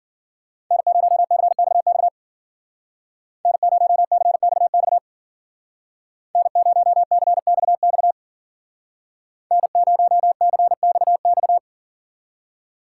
Morse code, heard as a keyed tone, about 49 wpm, 700 Hz. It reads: N0CXX N0CXX N0CXX N0CXX
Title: Speed Racing - USA 1x3 Call Signs 40wpm